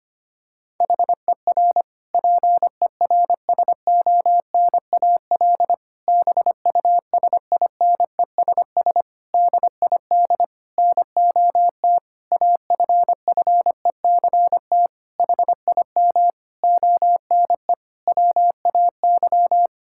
Morse code, heard as a keyed tone, about 25 wpm, 700 Hz. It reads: HER PERSONAL BUSINESS DID NOT AFFECT HIM ONE WAY
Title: Sentences from Top 1000 Words - Review Part 6 - 25wpm